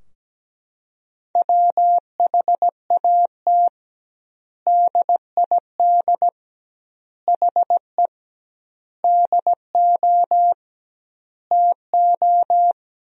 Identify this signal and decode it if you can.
Morse code, heard as a keyed tone, about 17 wpm, 700 Hz. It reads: WHAT DID HE DO TO
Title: 2x Word-Spacing - Sentences from Top 100 Words - 17wpm